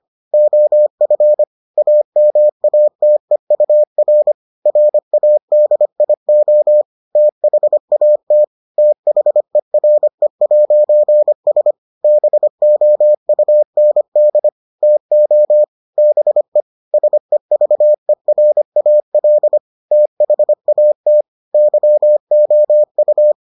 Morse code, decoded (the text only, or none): OF AMATEUR RADIO THAT THERE'S BOUND TO BE SEVERAL THAT YOU